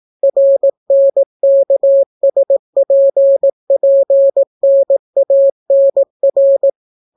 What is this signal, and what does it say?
Morse code, heard as a keyed tone, about 18 wpm, 550 Hz. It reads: RNKSPPNANR